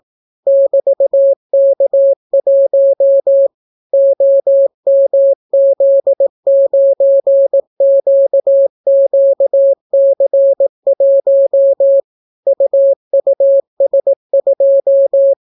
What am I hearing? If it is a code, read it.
Morse code, heard as a keyed tone, about 18 wpm, 550 Hz. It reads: <BT>K1 OMZ9QQC1 UUS2